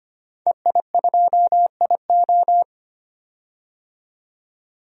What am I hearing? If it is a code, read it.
Morse code, heard as a keyed tone, about 25 wpm, 700 Hz. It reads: EI2IO
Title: DXCC-100 Call Signs - 5 Characters - 25wpm